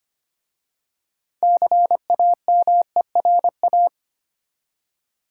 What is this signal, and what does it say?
Morse code, heard as a keyed tone, about 25 wpm, 700 Hz. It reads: CAMERA
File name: practice / 25wpm / Easy Words with Hints - 25wpm